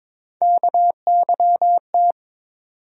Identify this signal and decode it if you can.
Morse code, heard as a keyed tone, about 22 wpm, 700 Hz. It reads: KYT
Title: USA Call Sign Postfixes - 3 Letters 22wpm